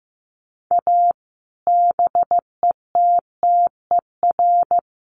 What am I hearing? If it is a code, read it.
Morse code, heard as a keyed tone, about 15 wpm, 700 Hz. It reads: A BETTER
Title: Sets of 5 Words Encore2 15wpm